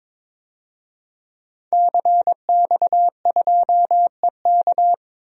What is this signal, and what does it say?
Morse code, heard as a keyed tone, about 23 wpm, 700 Hz. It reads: CX2EK
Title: DXCC-100 Call Signs - 5 Characters - 22wpm